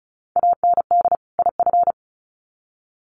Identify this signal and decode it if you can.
Morse code, heard as a keyed tone, about 34 wpm, 700 Hz. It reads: AND IF
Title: Sets of 2 Words 35wpm